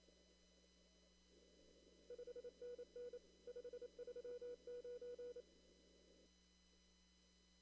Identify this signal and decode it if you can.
Morse code, heard as a keyed tone, about 27 wpm, 490 Hz. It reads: 5NN 539
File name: CW / kt5j wpx cw-op-k5pi